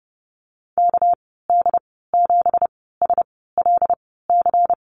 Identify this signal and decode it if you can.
Morse code, heard as a keyed tone, about 32 wpm, spaced as for 17 wpm, 700 Hz. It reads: KD7SLC
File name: Lesson 37g - Letter x - Call Signs 30wpm